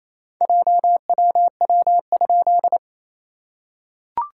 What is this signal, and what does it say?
Morse code, heard as a keyed tone, about 28 wpm, 690 Hz. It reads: JWW?
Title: SOTA QSO Practice 28wpm